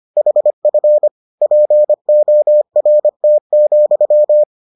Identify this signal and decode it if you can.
Morse code, heard as a keyed tone, about 25 wpm, 600 Hz. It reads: HF PORT,